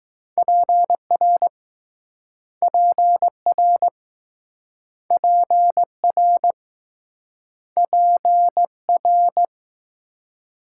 Morse code, heard as a keyed tone, about 21 wpm, 700 Hz. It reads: PR PR PR PR